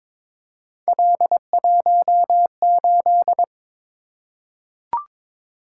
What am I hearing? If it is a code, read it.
Morse code, heard as a keyed tone, about 22 wpm, 700 Hz. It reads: L18
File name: practice / 22wpm / Three Letter-Number Combinations - 22wpm